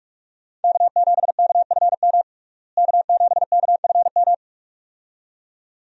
Morse code, heard as a keyed tone, about 41 wpm, 700 Hz. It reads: K7KFM K7KFM
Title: USA Call Signs - 5 Characters - Repeated 2x - 45wpm